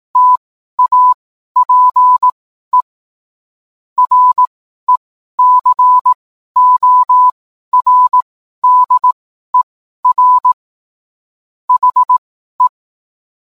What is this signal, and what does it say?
Morse code, heard as a keyed tone, about 18 wpm, spaced as for 12 wpm, 1000 Hz. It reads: TAPE RECORDER HE